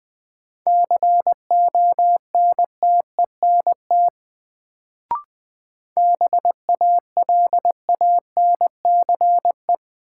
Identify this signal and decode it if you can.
Morse code, heard as a keyed tone, about 20 wpm, 700 Hz. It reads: CONTENT BALANCE
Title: Sentences from Top 900 Words - New Word Drill - 20wpm